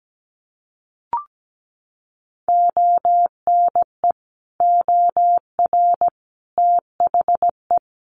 Morse code, heard as a keyed tone, about 17 wpm, 700 Hz. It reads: ONE OR THE
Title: Sentences from Top 400 Words 17wpm